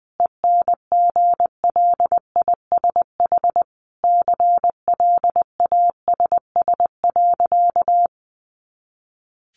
Morse code, heard as a keyed tone, about 20 wpm, 700 Hz. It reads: ENGLISH CLASS.
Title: Additional Easy Phrases 20wpm